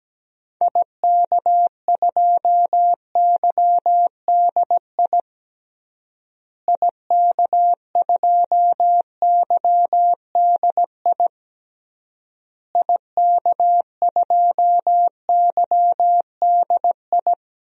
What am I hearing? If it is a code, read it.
Morse code, heard as a keyed tone, about 17 wpm, 700 Hz. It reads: IK2YDI IK2YDI IK2YDI
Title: DXCC-100 Call Signs - 6 Characters - Repeated 3x - 17wpm